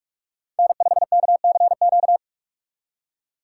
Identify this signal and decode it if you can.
Morse code, heard as a keyed tone, about 46 wpm, 700 Hz. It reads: N5KCQ
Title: USA Call Signs - 5 Characters - 2x Character Spacing - 45wpm